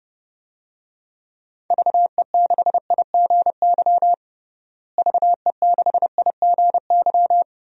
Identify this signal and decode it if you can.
Morse code, heard as a keyed tone, about 32 wpm, 700 Hz. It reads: VE6IGY VE6IGY
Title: DXCC-100 Call Signs - 6 Characters - Repeated 2x - 30wpm